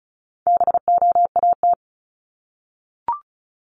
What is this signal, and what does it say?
Morse code, heard as a keyed tone, about 34 wpm, 700 Hz. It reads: BOAT
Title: Top 500 Words 35wpm